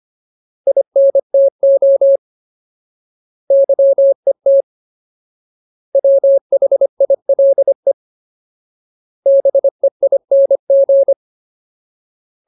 Morse code, heard as a keyed tone, about 25 wpm, 550 Hz. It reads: INTO YET WHILE BEING
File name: Ultimate Word List / 25WPM Ultimate